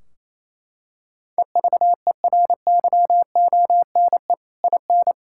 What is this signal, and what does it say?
Morse code, heard as a keyed tone, about 28 wpm, 700 Hz. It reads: EVERYONE IN
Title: Sentences from Top 1600 Words 28wpm